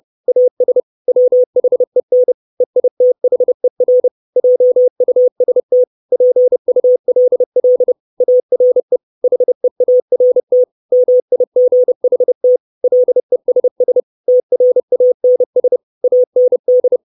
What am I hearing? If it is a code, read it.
Morse code, heard as a keyed tone, about 30 wpm, 500 Hz. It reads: AS WHEN EITHER JUST PULL ARE HEART MIGHT LESS TRANS AND
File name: 632 Words 30WPM